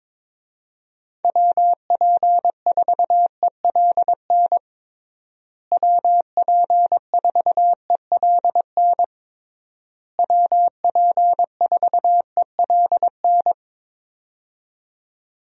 Morse code, heard as a keyed tone, about 22 wpm, 700 Hz. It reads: WP4ELN WP4ELN WP4ELN
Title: USA Call Signs - 6 Characters - Repeated 3x - 22wpm